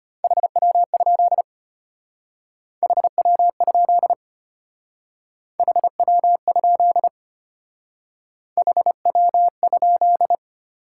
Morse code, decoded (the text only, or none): HW? HW? HW? HW?